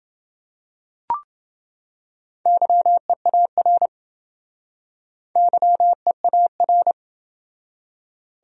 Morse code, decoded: YEAR YEAR

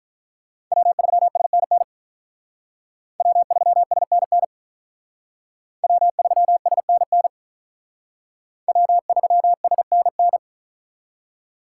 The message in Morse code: W3SNN W3SNN W3SNN W3SNN

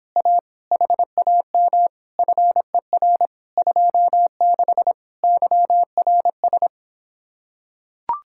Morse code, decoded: A HAM FER 26 YRS

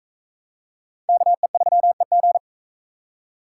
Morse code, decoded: KE3EG

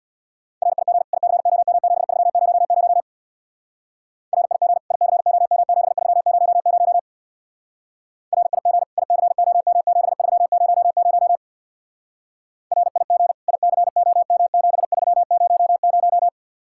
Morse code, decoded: RIG ICOM7300 RIG ICOM7300 RIG ICOM7300 RIG ICOM7300

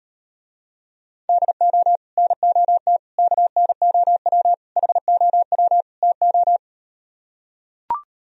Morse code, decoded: DO NOT KNOW HOW TO

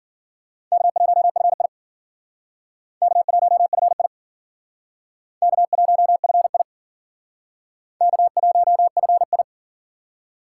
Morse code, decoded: K1FI K1FI K1FI K1FI

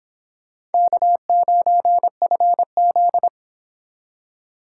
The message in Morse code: K9FZ

